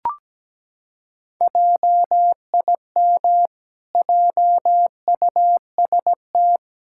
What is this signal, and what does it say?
Morse code, heard as a keyed tone, about 17 wpm, 700 Hz. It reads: JIM JUST
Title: Pangram Challenge v2 - 17wpm